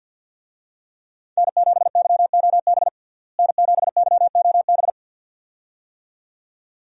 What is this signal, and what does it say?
Morse code, heard as a keyed tone, about 53 wpm, 700 Hz. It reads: N7YOB N7YOB
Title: USA Call Signs - 5 Characters - Repeated 2x - 50wpm